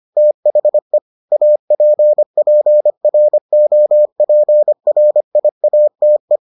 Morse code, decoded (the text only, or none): THE APPROPRIATE